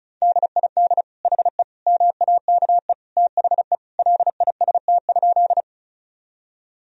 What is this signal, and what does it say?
Morse code, heard as a keyed tone, about 36 wpm, 700 Hz. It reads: DID HE MAKE THE LIST?
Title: Sentences from Top 1000 Words - Review Part 2 - 35wpm